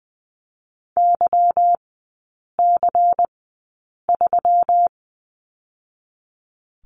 Morse code, heard as a keyed tone, about 19 wpm, 690 Hz. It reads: Y C 3
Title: 145 Lesson 24e - Letter y - Three Characters 20wpm